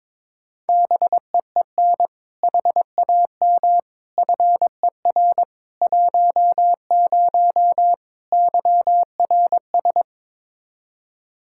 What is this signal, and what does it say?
Morse code, heard as a keyed tone, about 22 wpm, 700 Hz. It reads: BEEN HAM FER 10 YRS
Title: Practice QSO - Exchange 3 - More Info - 22wpm